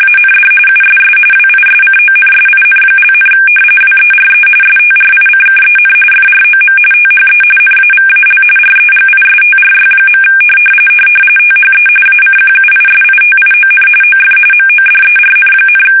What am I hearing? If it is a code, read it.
RTTY, about 45.45 baud. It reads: AND USS WISCONSIN), WERE STRICKEN FROM THE NAVAL REGISTRY. THE ACQUISITION OF MODERN,
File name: fsk850 / battleships